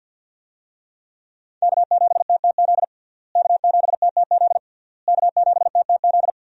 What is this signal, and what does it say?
Morse code, heard as a keyed tone, about 52 wpm, 700 Hz. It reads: K7TTZ K7TTZ K7TTZ
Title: USA Call Signs - 5 Characters - Repeated 3x - 50wpm